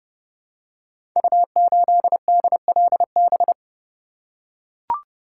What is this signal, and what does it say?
Morse code, heard as a keyed tone, about 30 wpm, 700 Hz. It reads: U8DLB
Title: Code Groups - 5 Letter:Numbers - 30wpm